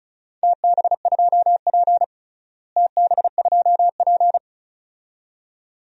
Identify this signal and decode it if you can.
Morse code, heard as a keyed tone, about 34 wpm, 700 Hz. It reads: TB2P TB2P
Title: DXCC-100 Call Signs - 4 Characters - Repeated 2x - 35wpm